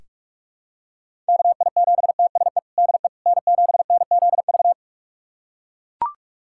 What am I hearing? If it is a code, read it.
Morse code, heard as a keyed tone, about 42 wpm, 700 Hz. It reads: KI7TSE DE N7NZV
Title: USA Callsign DE Callsign 45wpm